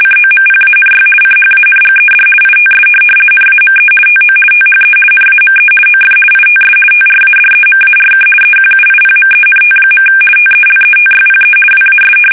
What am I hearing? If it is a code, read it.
RTTY, about 50 baud. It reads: COMPANY CHANGED ITS NAME TO THE NAME OF ITS TRADEMARK EQUIPMENT. TELETYPE WAS
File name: fsk850-50B / teletype-corp